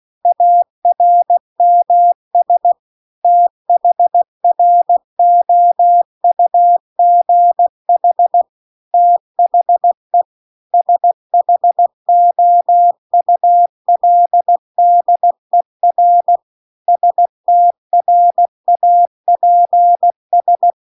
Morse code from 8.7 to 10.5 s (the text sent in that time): THE